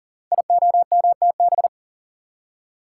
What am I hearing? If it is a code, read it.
Morse code, heard as a keyed tone, about 42 wpm, 700 Hz. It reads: IOMTB
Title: Code Groups - 5 Letters - 40wpm